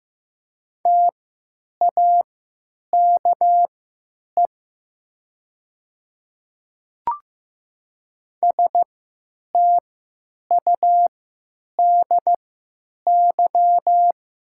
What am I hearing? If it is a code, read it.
Morse code, heard as a keyed tone, about 15 wpm, spaced as for 9 wpm, 700 Hz. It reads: TAKE STUDY
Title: 3x Character Spacing - Top 300 Words - 15wpm